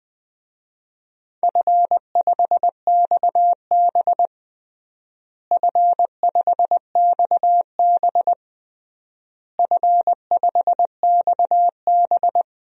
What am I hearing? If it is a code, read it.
Morse code, heard as a keyed tone, about 20 wpm, 700 Hz. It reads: F5XB F5XB F5XB
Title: DXCC-100 Call Signs - 4 Characters - Repeated 3x - 20wpm